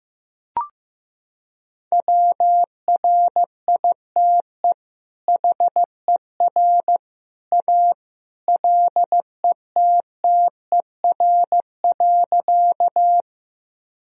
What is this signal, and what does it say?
Morse code, heard as a keyed tone, about 15 wpm, 700 Hz. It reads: WRITE HER A LETTER.
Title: Sentences from Top 1000 Words - Review Part 1 - 15wpm